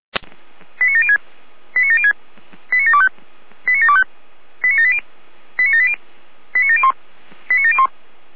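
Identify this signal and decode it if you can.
Five-tone selective call (ZVEI), heard as quick runs of five tones, tones of about 80 ms. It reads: 89706 89706 89725 89725 89780 89780 89701 89701
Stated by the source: Five-tone ZVEI selective call 89426, received by SDR.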